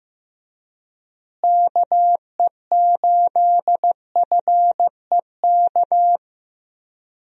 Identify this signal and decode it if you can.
Morse code, heard as a keyed tone, about 15 wpm, 700 Hz. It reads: KE8FEK
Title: USA Call Signs - 6 Characters - Repeated 3x - 15wpm